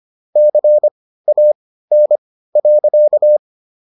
Morse code, heard as a keyed tone, about 25 wpm, spaced as for 15 wpm, 600 Hz. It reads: CAN.